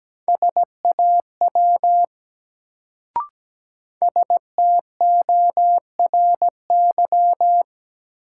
Morse code, heard as a keyed tone, about 17 wpm, 700 Hz. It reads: SAW STORY
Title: Sentences from Top 300 Words - New Word Drill - 17wpm